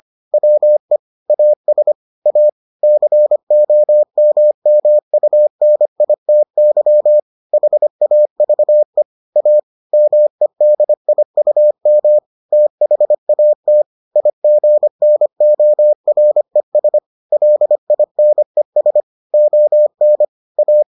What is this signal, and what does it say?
Morse code, heard as a keyed tone, about 25 wpm, 600 Hz. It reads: WE AS A COMMUNITY HAVE A MEDIUM THAT IGNORES LINES ON A